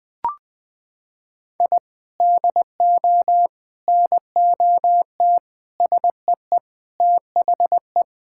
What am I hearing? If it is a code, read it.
Morse code, heard as a keyed tone, about 20 wpm, 700 Hz. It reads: I DO NOT SEE THE
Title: Sentences with 4-Word Phrases - 20wpm